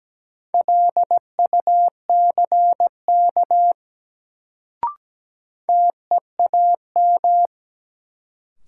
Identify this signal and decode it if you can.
Morse code, heard as a keyed tone, about 17 wpm, 700 Hz. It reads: LUCK TEAM
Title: Sentences from Top 1500 Words - New Word Drill Alt - 17wpm